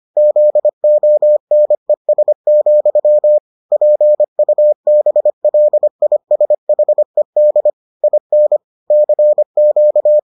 Morse code, heard as a keyed tone, about 25 wpm, 600 Hz. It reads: ZONES, PUBLISHED IN CQ